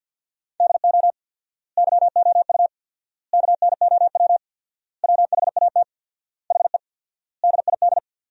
Morse code, heard as a keyed tone, about 53 wpm, 700 Hz. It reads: DO YOU KNOW WHAT HE DID
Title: 4x Word-Spacing - Sentences from Top 300 Words - 50wpm